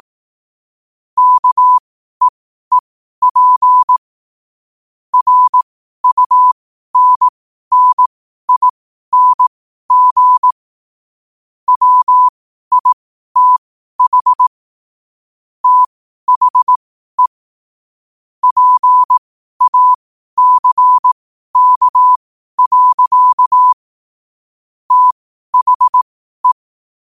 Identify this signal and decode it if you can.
Morse code, heard as a keyed tone, about 18 wpm, spaced as for 12 wpm, 1000 Hz. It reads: KEEP RUNNING WITH THE PACK. THE
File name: qotdfiles / 12wpm